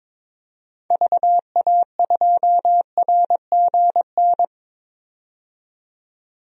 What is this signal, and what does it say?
Morse code, heard as a keyed tone, about 22 wpm, 700 Hz. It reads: VA2RGN
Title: DXCC-100 Call Signs - 6 Characters - 22wpm